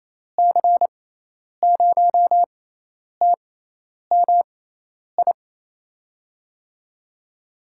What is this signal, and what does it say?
Morse code, heard as a keyed tone, about 28 wpm, spaced as for 10 wpm, 700 Hz. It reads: C0TMI
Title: USA Call Signs - 6 Characters - 6x Character Spacing - 28wpm